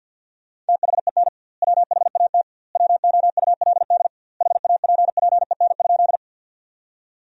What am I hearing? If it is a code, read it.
Morse code, heard as a keyed tone, about 48 wpm, 700 Hz. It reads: THEN WHAT WOULD HAPPEN?
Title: 2x Word-Spacing - Sentences from Top 400 Words - 50wpm